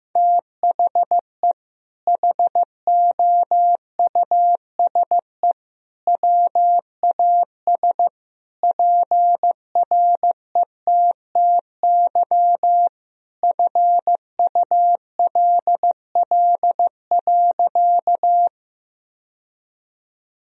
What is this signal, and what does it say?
Morse code, heard as a keyed tone, about 16 wpm, 700 Hz. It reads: THE HOUSE WAS PRETTY FULL.
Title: Sentences from Top 1000 Words - Review Part 3 - 15wpm